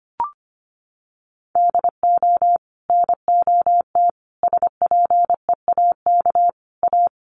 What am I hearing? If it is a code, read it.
Morse code, heard as a keyed tone, about 25 wpm, 700 Hz. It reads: DO NOT SPEAK A
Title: Sentences from Top 1100 Words 25wpm